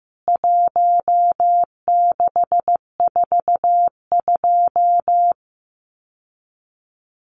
Morse code, heard as a keyed tone, about 15 wpm, 700 Hz. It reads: 1642
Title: CWT Exchange 15wpm